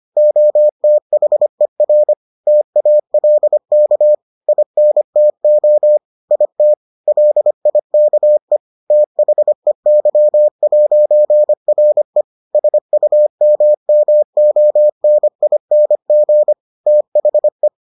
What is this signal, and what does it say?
Morse code, heard as a keyed tone, about 25 wpm, 600 Hz. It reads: OTHER TALK INTO IT LIKE THEY'RE SUMMONING THE